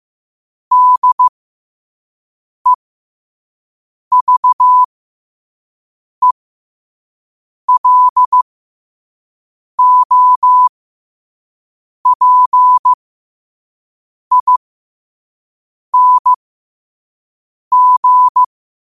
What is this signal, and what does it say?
Morse code, heard as a keyed tone, about 15 wpm, spaced as for 5 wpm, 1000 Hz. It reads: DEVELOPING
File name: 5wpm